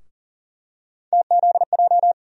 Morse code, heard as a keyed tone, about 40 wpm, 700 Hz. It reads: TZJ
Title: USA Call Sign Postfixes - 3 Letters 40wpm